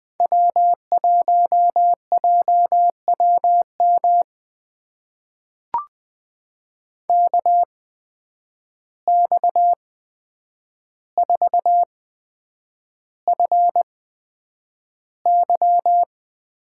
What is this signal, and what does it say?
Morse code, heard as a keyed tone, about 20 wpm, spaced as for 9 wpm, 700 Hz. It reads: W1JWM KX4FY
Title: USA Call Signs - 5 Characters - 8x Character Spacing - 20wpm